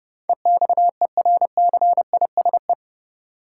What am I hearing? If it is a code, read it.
Morse code, heard as a keyed tone, about 30 wpm, 700 Hz. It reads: EXERCISE
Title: Sentences from Top 1000 Words - Review Part 4 - New Word Drill - 30wpm